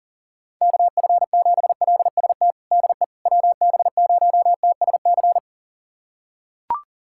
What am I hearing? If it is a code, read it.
Morse code, heard as a keyed tone, about 40 wpm, 700 Hz. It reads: KF7LST DE WB0TSC
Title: USA Callsign DE Callsign 40wpm